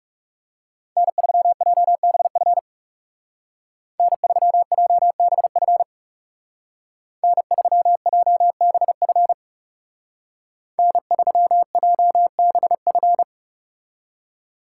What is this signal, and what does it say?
Morse code, heard as a keyed tone, about 36 wpm, 700 Hz. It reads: N3JBF N3JBF N3JBF N3JBF
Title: Speed Racing - USA 1x3 Call Signs 30wpm